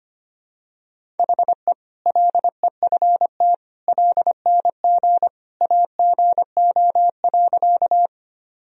Morse code, heard as a keyed tone, about 25 wpm, 700 Hz. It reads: HE LEFT LNG AGO.
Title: Sentences from Top 500 Words and Abbreviations 25wpm